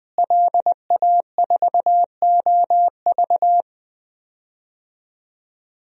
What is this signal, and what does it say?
Morse code, heard as a keyed tone, about 21 wpm, 700 Hz. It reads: LA4OV
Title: DXCC-100 Call Signs - 5 Characters - Repeated 3x - 20wpm